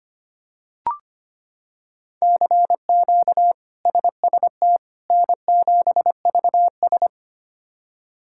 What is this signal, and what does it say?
Morse code, heard as a keyed tone, about 25 wpm, 700 Hz. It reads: CQ SST N7VS
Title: K1USN SST QSO Practice - 25wpm